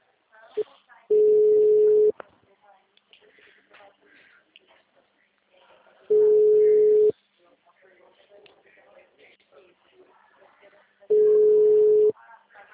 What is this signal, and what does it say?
Ringback tone (Europe, 425 Hz), with speech.
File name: out- -2011- - - .820